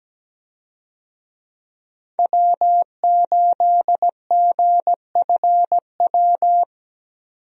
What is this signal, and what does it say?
Morse code, heard as a keyed tone, about 17 wpm, 700 Hz. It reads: W8GFW
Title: USA Call Signs - 5 Characters - 17wpm